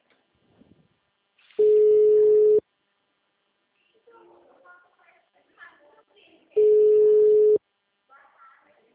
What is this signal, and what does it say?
Ringback tone (Europe, 425 Hz), with speech.
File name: out- -2006- - - .424